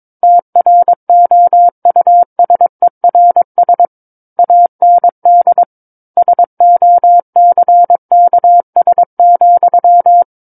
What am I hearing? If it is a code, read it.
Morse code, heard as a keyed tone, about 22 wpm, 700 Hz. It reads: TROUSERS AND SOCKS,
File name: Book One / 13-Chapter 13 - 22 WPM